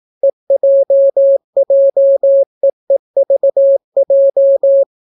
Morse code, heard as a keyed tone, about 18 wpm, 550 Hz. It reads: EJJEEVJ